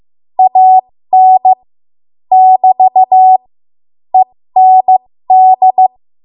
Morse code, heard as a keyed tone, about 15 wpm, spaced as for 13 wpm, 750 Hz. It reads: AN <BT> END
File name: Morse / 13WPM